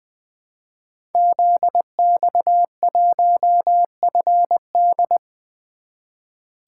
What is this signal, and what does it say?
Morse code, heard as a keyed tone, about 20 wpm, 700 Hz. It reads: ZX1FD
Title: Code Groups - 5 Letter:Numbers - 20wpm